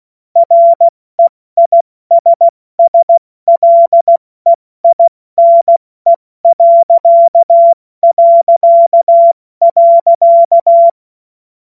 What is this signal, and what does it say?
Morse code, heard as a keyed tone, about 16 wpm, 680 Hz. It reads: REISSLEINE...
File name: Morsekurs / Witze / Witz 31.Oktober 2023 70 F80